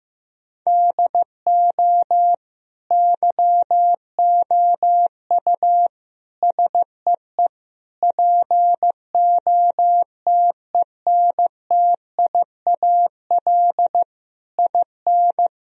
Morse code, heard as a keyed tone, about 15 wpm, 700 Hz. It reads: DO YOU SEE POTENTIAL IN
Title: Sentences from Top 1000 Words - Review Part 4 - 15wpm